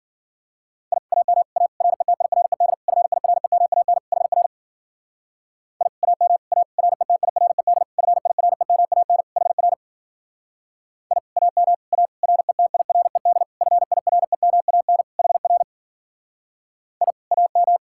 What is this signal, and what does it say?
Morse code, heard as a keyed tone, about 52 wpm, 700 Hz. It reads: I AM A RETIRED FIREMAN HR I AM A RETIRED FIREMAN HR I AM A RETIRED FIREMAN HR I AM